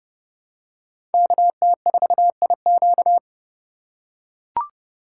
Morse code, heard as a keyed tone, about 29 wpm, 700 Hz. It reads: KT4IQ